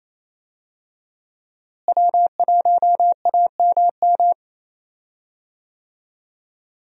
Morse code, heard as a keyed tone, about 28 wpm, 700 Hz. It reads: W1AMM
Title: USA Call Signs - 5 Characters - Repeated 3x - 28wpm